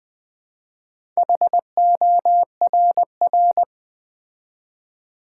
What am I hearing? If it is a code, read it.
Morse code, heard as a keyed tone, about 20 wpm, 700 Hz. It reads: HORR
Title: Build A Word - 2 Letters At A Time - 8-Letter Words - 20wpm